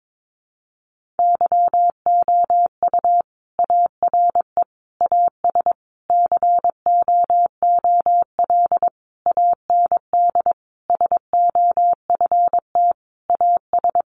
Morse code, heard as a keyed tone, about 22 wpm, 700 Hz. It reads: YOU ARE AS COOL AND SOFT AS
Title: Sentences from Top 1500 Words 22wpm